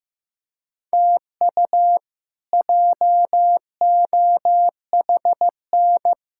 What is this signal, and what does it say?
Morse code, heard as a keyed tone, about 15 wpm, 700 Hz. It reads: TU JOHN
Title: K1USN SST QSO Practice - 15wpm